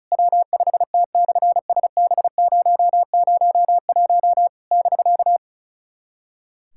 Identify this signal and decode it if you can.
Morse code, heard as a keyed tone, about 34 wpm, 700 Hz. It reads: W5T/SB001 <BK>
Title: SOTA QSO Practice 35wpm